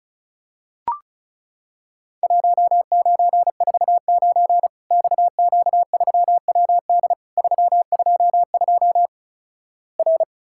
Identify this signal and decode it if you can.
Morse code, heard as a keyed tone, about 36 wpm, 700 Hz. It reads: 1949 XQ3WD 322 R
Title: Worked All Europe DX Contest - QTC Exchange - 35wpm